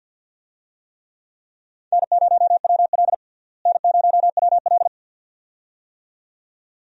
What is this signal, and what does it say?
Morse code, heard as a keyed tone, about 52 wpm, 700 Hz. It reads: N0WL N0WL
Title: USA Call Signs - 4 Characters - Repeated 2x - 50wpm